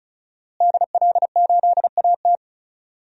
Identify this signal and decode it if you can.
Morse code, heard as a keyed tone, about 34 wpm, 700 Hz. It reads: DL8AT
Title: DXCC-100 Call Signs - 5 Characters - Repeated 3x - 35wpm